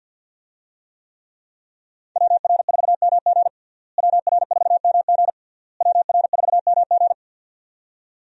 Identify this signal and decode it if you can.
Morse code, heard as a keyed tone, about 51 wpm, 700 Hz. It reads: WR4MG WR4MG WR4MG
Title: USA Call Signs - 5 Characters - Repeated 3x - 50wpm